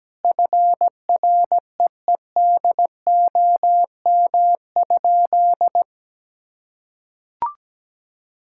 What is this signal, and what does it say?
Morse code, heard as a keyed tone, about 17 wpm, 700 Hz. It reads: FREEDOM?
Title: Sentences from Top 1600 Words 17wpm